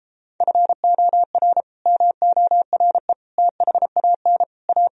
Morse code, heard as a keyed tone, about 33 wpm, 700 Hz. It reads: FOR MORE THAN A